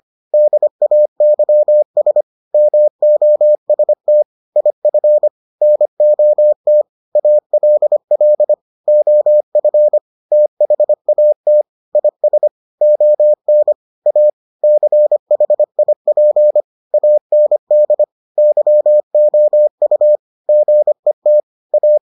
Morse code, heard as a keyed tone, about 25 wpm, 600 Hz. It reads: DAYS MOST IF NOT ALL OF THAT IS ON A CHIP AND YOU GET A